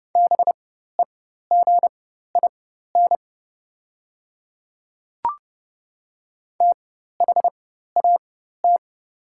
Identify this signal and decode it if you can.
Morse code, heard as a keyed tone, about 30 wpm, spaced as for 14 wpm, 700 Hz. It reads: BEGIN THAT
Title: 4x Character Spacing - Top 300 Words - 30wpm